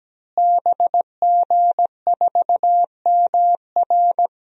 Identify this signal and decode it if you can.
Morse code, heard as a keyed tone, about 17 wpm, 700 Hz. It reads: BG4MR
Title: DXCC-100 Call Signs - 5 Characters - Repeated 2x - 17wpm